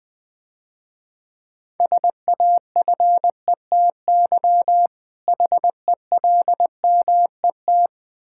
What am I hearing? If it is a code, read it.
Morse code, heard as a keyed tone, about 22 wpm, 700 Hz. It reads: SAFETY HELMET
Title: Difficult Words - Speed-Racing with Hints - 20wpm